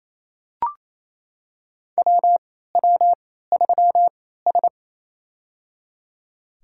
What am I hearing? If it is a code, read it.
Morse code, heard as a keyed tone, about 28 wpm, spaced as for 16 wpm, 700 Hz. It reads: WW3S
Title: USA Call Signs - 4 Characters - 3x Character Spacing - 28wpm